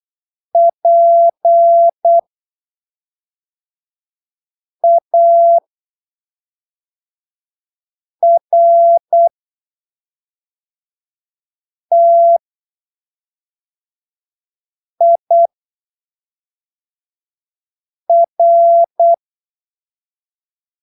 Morse code, heard as a keyed tone, about 8 wpm, spaced as for 3 wpm, 680 Hz. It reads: PARTIR
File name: arquivos / texto / texto 15 08ppm